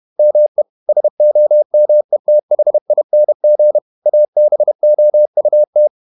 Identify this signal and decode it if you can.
Morse code, heard as a keyed tone, about 31 wpm, 600 Hz. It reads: ME SOMETHING ABOUT
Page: Never 31wpm | CW med Gnister